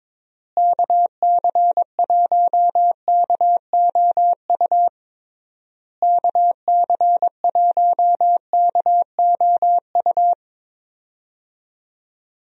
Morse code, heard as a keyed tone, about 22 wpm, 700 Hz. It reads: KC1KOU KC1KOU
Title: USA Call Signs - 6 Characters - Repeated 3x - 22wpm